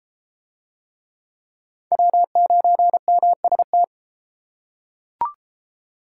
Morse code, heard as a keyed tone, about 33 wpm, 700 Hz. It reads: W9MST